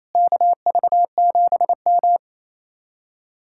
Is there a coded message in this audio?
Yes: Morse code from 0.1 to 2.2 s, heard as a keyed tone, about 36 wpm, 700 Hz. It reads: KV7M